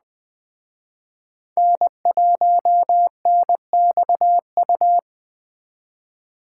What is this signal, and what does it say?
Morse code, heard as a keyed tone, about 21 wpm, 700 Hz. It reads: N1NXU